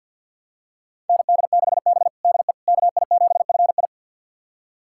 Morse code, heard as a keyed tone, about 49 wpm, 700 Hz. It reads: ND6B DE KI7FI
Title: USA Callsign DE Callsign 50wpm